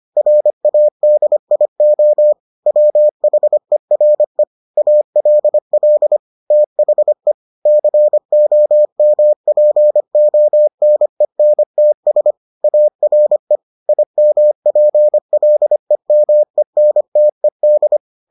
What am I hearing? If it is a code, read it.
Morse code, heard as a keyed tone, about 25 wpm, 600 Hz. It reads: RADIO WHERE ALL THE COMPONENTS ARE IMPLEMENTED